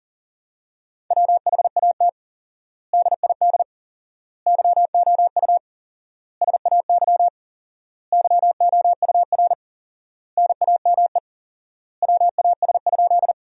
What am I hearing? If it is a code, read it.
Morse code, heard as a keyed tone, about 40 wpm, 700 Hz. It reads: WHAT DID YOU SAY YOUR NAME WAS?
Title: 4x Word-Spacing - Sentences from Top 200 Words - 40wpm